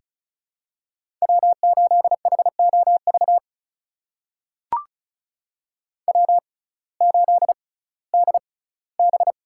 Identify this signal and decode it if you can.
Morse code, heard as a keyed tone, about 34 wpm, spaced as for 19 wpm, 700 Hz. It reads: W8HOV W8DB
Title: USA Call Signs - 6x Character Spacing - 35wpm